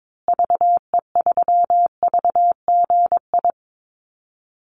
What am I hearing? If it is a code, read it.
Morse code, heard as a keyed tone, about 22 wpm, 700 Hz. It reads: VE3VGI
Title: Call Signs Repeated 2x - 22wpm